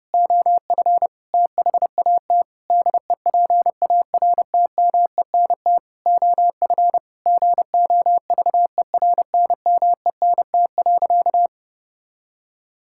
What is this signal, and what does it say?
Morse code, heard as a keyed tone, about 32 wpm, 700 Hz. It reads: OF THAT DEPARTMENT OF GOVERNMENT.
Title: Sentences from Top 1000 Words - Review Part 5 - 30wpm